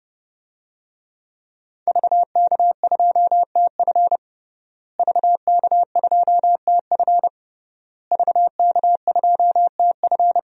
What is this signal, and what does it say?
Morse code, heard as a keyed tone, about 30 wpm, 700 Hz. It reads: VK2TF VK2TF VK2TF
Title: DXCC-100 Call Signs - 5 Characters - Repeated 3x - 30wpm